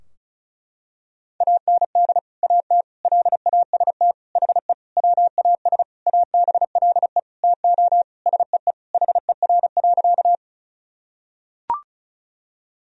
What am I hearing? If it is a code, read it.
Morse code, heard as a keyed tone, about 36 wpm, 700 Hz. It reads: AND AT LAST HE WAS ABLE TO SEE HER.
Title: Sentences from Top 1000 Words - Review Part 2 - 35wpm